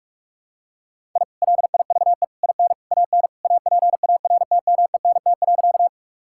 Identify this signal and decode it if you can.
Morse code, heard as a keyed tone, about 46 wpm, 700 Hz. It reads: I LIVE IN AN APARTMENT.
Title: Sentences from Top 1000 Words - Review Part 6 - 45wpm